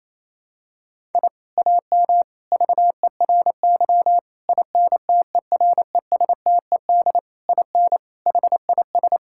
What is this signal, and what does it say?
Morse code, heard as a keyed tone, about 28 wpm, 700 Hz. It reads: I AM VERY INTERESTED IN HIS
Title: Sentences from Top 1000 Words - Review Part 6 - 28wpm